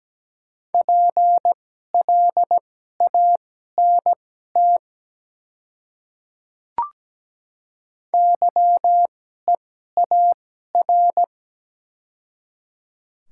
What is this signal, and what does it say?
Morse code, heard as a keyed tone, about 17 wpm, spaced as for 12 wpm, 690 Hz. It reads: PLANT YEAR
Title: 2x Character Spacing - Top 300 Words - 17wpm